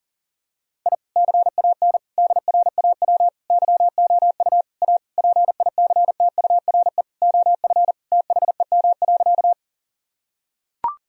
Morse code, heard as a keyed tone, about 40 wpm, 700 Hz. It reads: I CAN DRAW YOU A PICTURE OF THEM.
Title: Sentences from Top 1000 Words - Review Part 1 - 40wpm